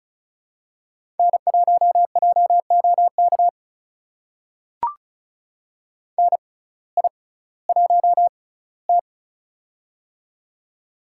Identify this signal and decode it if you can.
Morse code, heard as a keyed tone, about 36 wpm, spaced as for 19 wpm, 700 Hz. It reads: N1JOK NI1T
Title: USA Call Signs - 6x Character Spacing - 35wpm